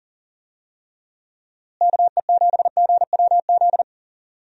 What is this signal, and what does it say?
Morse code, heard as a keyed tone, about 37 wpm, 700 Hz. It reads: KE7GWZ
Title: USA Call Signs - 6 Characters - Repeated 3x - 40wpm